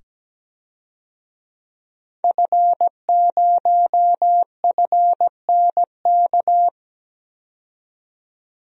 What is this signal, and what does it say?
Morse code, heard as a keyed tone, about 17 wpm, 700 Hz. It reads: F0FNK
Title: DXCC-100 Call Signs - 5 Characters - 17wpm